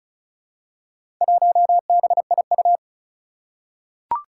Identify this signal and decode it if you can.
Morse code, heard as a keyed tone, about 36 wpm, 700 Hz. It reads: 1BIU
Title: Four Challenging Letter-Number Combinations 35wpm